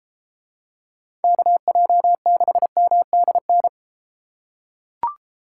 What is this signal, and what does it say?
Morse code, heard as a keyed tone, about 33 wpm, 700 Hz. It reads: KJ6MDN